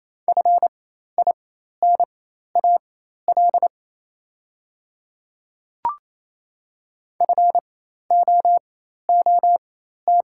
Morse code, heard as a keyed tone, about 27 wpm, spaced as for 13 wpm, 700 Hz. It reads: FINAL FOOT
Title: 4x Character Spacing - Top 500 Words - 28wpm